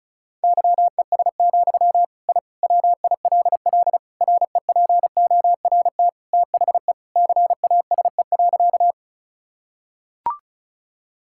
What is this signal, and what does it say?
Morse code, heard as a keyed tone, about 34 wpm, 700 Hz. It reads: YES, I WILL REPORT THE CASE.
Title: Sentences from Top 1000 Words - Review Part 6 - 35wpm